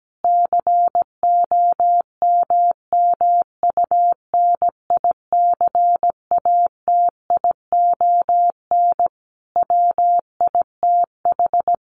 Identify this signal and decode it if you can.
Morse code, heard as a keyed tone, about 17 wpm, 700 Hz. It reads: COMMUNICATION WITH
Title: Sentences from Top 1400 Words 17wpm